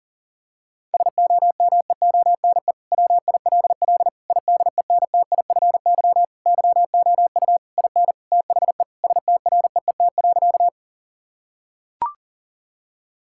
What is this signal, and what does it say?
Morse code, heard as a keyed tone, about 40 wpm, 700 Hz. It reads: SOMEONE WILL IDENTIFY YOU IN THE STREET.
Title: Sentences from Top 1000 Words - Review Part 6 - 40wpm